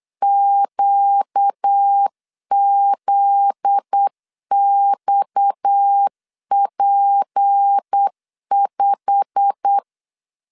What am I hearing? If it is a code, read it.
Morse code, heard as a keyed tone, about 8 wpm, 800 Hz. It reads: QZXP5
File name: LEZIONI / 9 28 8wpm 3512